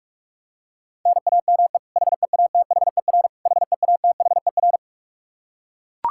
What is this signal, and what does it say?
Morse code, heard as a keyed tone, about 41 wpm, 700 Hz. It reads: NAME HEATHER HEATHER
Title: K1USN SST QSO Practice 45wpm